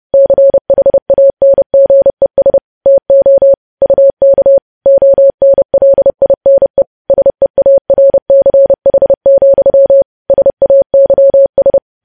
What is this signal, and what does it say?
Morse code, heard as a keyed tone, about 30 wpm, 550 Hz. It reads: CHANGES TO UK ONLINE SEARCH, SAYS
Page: news headlines in morse code at 30 wpm